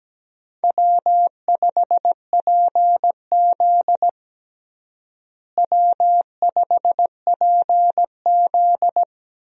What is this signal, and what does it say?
Morse code, heard as a keyed tone, about 17 wpm, 700 Hz. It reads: W5PZ W5PZ
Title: USA Call Signs - 4 Characters - Repeated 3x - 17wpm